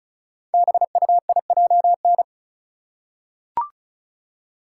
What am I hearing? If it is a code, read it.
Morse code, heard as a keyed tone, about 36 wpm, 700 Hz. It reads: BUIJN
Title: Code Groups - 5 Letters - 35wpm